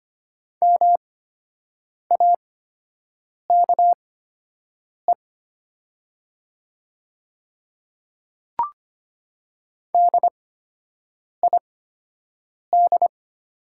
Morse code, heard as a keyed tone, about 25 wpm, spaced as for 7 wpm, 700 Hz. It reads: MAKE DID
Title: 8x Character Spacing - Top 200 Words - 25wpm